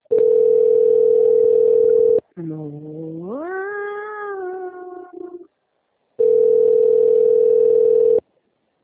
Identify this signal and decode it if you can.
Ringback tone (North America), 440 + 480 Hz.
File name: out- - - - - .41389